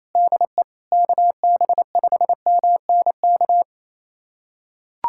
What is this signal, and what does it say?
Morse code, heard as a keyed tone, about 27 wpm, 700 Hz. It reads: DE KB5MNK
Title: USA Callsign DE Callsign - 6 Characters - 28wpm